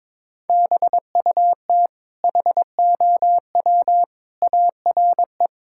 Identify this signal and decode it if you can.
Morse code, heard as a keyed tone, about 22 wpm, 700 Hz. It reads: BUT HOW ARE
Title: Sentences from Top 1000 Words - Review Part 6 - 22wpm